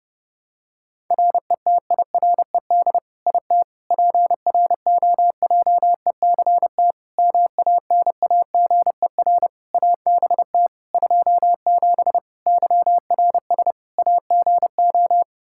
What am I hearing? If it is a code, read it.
Morse code, heard as a keyed tone, about 30 wpm, 700 Hz. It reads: RETIRED IT PROJECT MANAGER ABT 27 YRS AGO
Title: Practice QSO - Exchange 3 - More Info - 30wpm